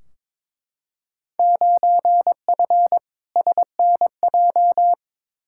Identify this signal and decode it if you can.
Morse code, heard as a keyed tone, about 22 wpm, 700 Hz. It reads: 9F SNJ
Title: Field Day Exchanges Encore - 22wpm